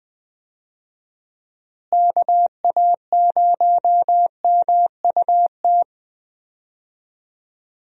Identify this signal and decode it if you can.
Morse code, heard as a keyed tone, about 20 wpm, 700 Hz. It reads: KA0MUT
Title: USA Call Signs - 6 Characters - Repeated 3x - 20wpm